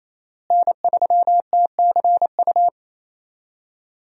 Morse code, heard as a keyed tone, about 27 wpm, 700 Hz. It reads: N3TCU
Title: USA Call Signs - 5 Characters - Repeated 2x - 28wpm